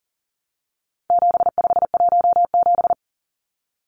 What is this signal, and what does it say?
Morse code, heard as a keyed tone, about 40 wpm, 700 Hz. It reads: 7517
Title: Numbers - 4-Digit 40wpm